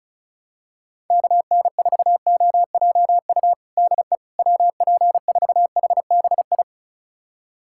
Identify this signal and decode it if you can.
Morse code, heard as a keyed tone, about 34 wpm, 690 Hz. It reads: KN4OJU DE WP4HBI
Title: USA Callsign DE Callsign 35wpm